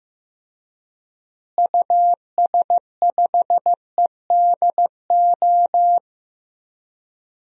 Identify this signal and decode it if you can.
Morse code, heard as a keyed tone, about 15 wpm, 700 Hz. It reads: US5EDO
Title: DXCC-100 Call Signs - 6 Characters - Repeated 2x - 15wpm